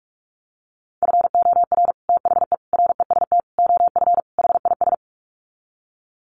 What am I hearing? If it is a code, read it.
Morse code, heard as a keyed tone, about 44 wpm, 700 Hz. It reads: FOR THE REST OF HIS
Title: Sets of 5 Words 45wpm